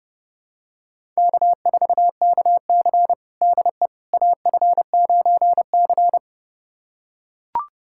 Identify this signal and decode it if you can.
Morse code, heard as a keyed tone, about 30 wpm, 700 Hz. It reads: K4KC DE AF9C
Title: USA Callsign DE Callsign - 4 Characters - 30wpm